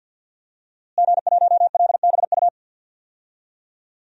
Morse code, heard as a keyed tone, about 52 wpm, 700 Hz. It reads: G1LBU
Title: DXCC-100 Call Signs - 5 Characters - Repeated 2x - 50wpm